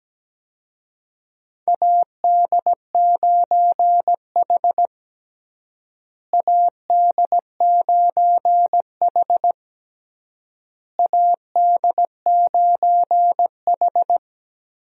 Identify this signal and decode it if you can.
Morse code, heard as a keyed tone, about 17 wpm, 700 Hz. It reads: AD9H AD9H AD9H
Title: USA Call Signs - 4 Characters - Repeated 3x - 17wpm